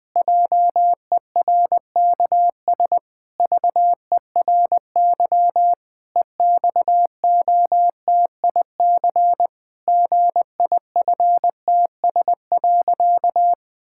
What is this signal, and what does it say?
Morse code, heard as a keyed tone, about 20 wpm, 700 Hz. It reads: JERKS VERY EXOTIC GIFTS.
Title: Pangram Challenge v2 - 20wpm